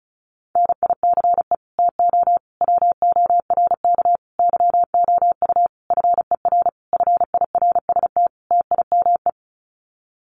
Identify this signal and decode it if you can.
Morse code, heard as a keyed tone, about 34 wpm, 700 Hz. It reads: NICE TO WORK YOU FER FIRST TIME
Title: Common QSO Phrases 35wpm